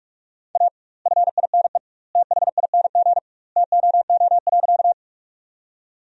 Morse code, heard as a keyed tone, about 41 wpm, 700 Hz. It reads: A FINE THING TOO.
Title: 2x Word-Spacing - Sentences from Top 500 Words - 45wpm